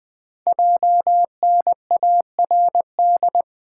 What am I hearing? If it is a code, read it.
Morse code, heard as a keyed tone, about 20 wpm, 700 Hz. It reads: JNARD
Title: Code Groups - 5 Letters - 20wpm